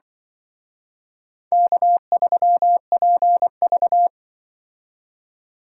Morse code, heard as a keyed tone, about 25 wpm, 700 Hz. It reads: K3PV
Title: Speed Racing - USA 1x2 Call Signs 20wpm